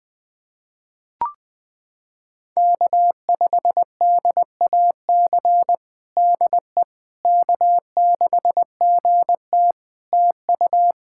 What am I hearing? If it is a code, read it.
Morse code, heard as a keyed tone, about 20 wpm, 700 Hz. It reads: K5DAC DE K6GT TU
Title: Practice QSO - Exchange 4 - Ending - 20wpm